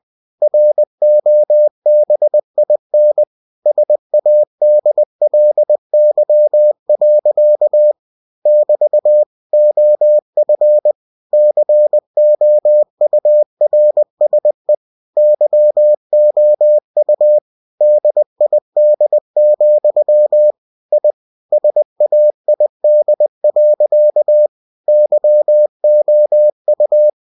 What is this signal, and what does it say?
Morse code, heard as a keyed tone, about 20 wpm, 600 Hz. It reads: ROBIN SADLY. <BT> OF COURSE YOU DID, I SAID. YOU
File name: cw-books / winnie-the-pooh / winnie-20wpm-page-0058